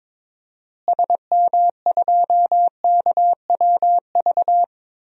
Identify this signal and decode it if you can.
Morse code, heard as a keyed tone, about 22 wpm, 700 Hz. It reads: SM2KWV
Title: DXCC-100 Call Signs - 6 Characters - Repeated 2x - 22wpm